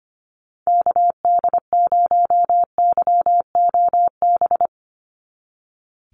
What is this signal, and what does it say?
Morse code, heard as a keyed tone, about 25 wpm, 700 Hz. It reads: KD0YOB
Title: NA SOTA Activator Callsigns 25wpm